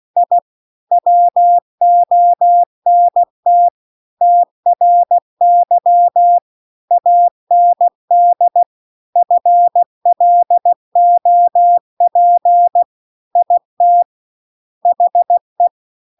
Morse code, heard as a keyed tone, about 16 wpm, 700 Hz. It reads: I WONT TRY AND FLOP IT HE